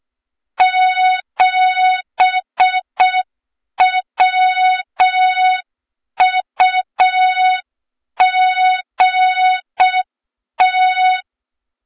Morse code, heard as a keyed tone, about 6 wpm, 3700 Hz, with noise. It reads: 7WUGT
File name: Morze-HA2MN / hang / 23 GYAKORLAT